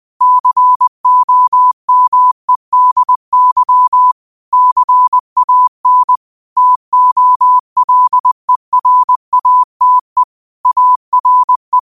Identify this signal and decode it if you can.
Morse code, heard as a keyed tone, about 20 wpm, 1000 Hz. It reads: COMEDY CAN TOLERATE ARE
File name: qotdfiles / 20wpm